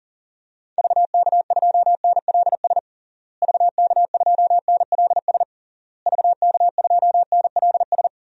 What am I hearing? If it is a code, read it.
Morse code, heard as a keyed tone, about 40 wpm, 700 Hz. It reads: VK2NLS VK2NLS VK2NLS
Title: DXCC-100 Call Signs - 6 Characters - Repeated 3x - 40wpm